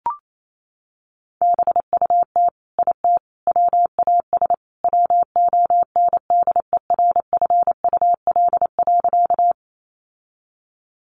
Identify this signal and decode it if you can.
Morse code, heard as a keyed tone, about 28 wpm, 700 Hz. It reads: BUT IT WAS WONDERFUL.
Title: Sentences from Top 1100 Words 28wpm